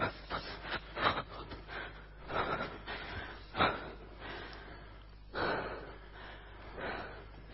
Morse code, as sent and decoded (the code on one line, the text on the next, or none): none